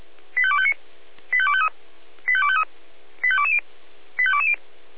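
Five-tone selective call (ZVEI), heard as quick runs of five tones, tones of about 70 ms. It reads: 85259 85253 85253 85220 85220
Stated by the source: ZVEI five-tone call 85270, received by SDR.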